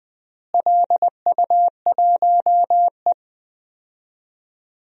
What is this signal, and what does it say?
Morse code, heard as a keyed tone, about 20 wpm, 700 Hz. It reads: LU1E
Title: DXCC-100 Call Signs - 4 Characters - Repeated 3x - 20wpm